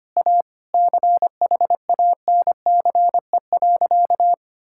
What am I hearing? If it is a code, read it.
Morse code, heard as a keyed tone, about 25 wpm, 700 Hz. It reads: A CHANCE.
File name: practice / 25wpm / Sentences with 4-Word Phrases - 25wpm